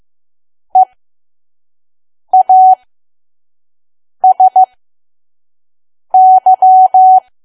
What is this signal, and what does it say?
Morse code, heard as a keyed tone, about 15 wpm, spaced as for 5 wpm, 750 Hz. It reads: EASY
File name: Archive / 5 WPM files / PM